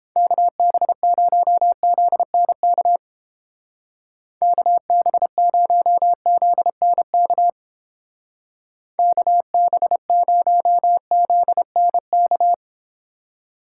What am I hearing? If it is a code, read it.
Morse code, heard as a keyed tone, about 29 wpm, 700 Hz. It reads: KB0ZNK KB0ZNK KB0ZNK